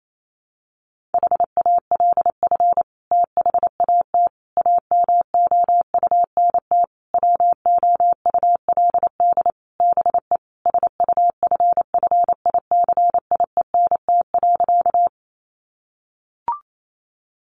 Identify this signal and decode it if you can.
Morse code, heard as a keyed tone, about 28 wpm, 700 Hz. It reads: HALF THAT AMOUNT WOULD BE SUFFICIENT.
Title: Sentences from Top 1400 Words 28wpm